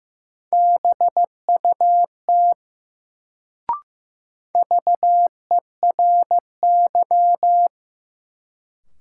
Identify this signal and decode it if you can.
Morse code, heard as a keyed tone, about 15 wpm, 700 Hz. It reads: BUT VERY
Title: Sets of 2 Words - New Word Drill - 15wpm